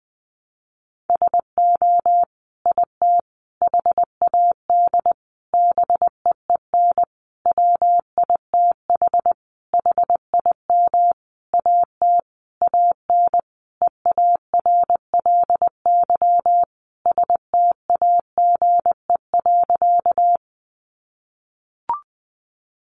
Morse code, heard as a keyed tone, about 20 wpm, 700 Hz. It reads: SO IT HAD BEEN WITH HIM AT AN EARLY STAGE.
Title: Sentences from Top 1200 Words 20wpm